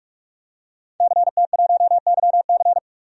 Morse code, heard as a keyed tone, about 45 wpm, 700 Hz. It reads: CT1YC
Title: DXCC-100 Call Signs - 5 Characters - 45wpm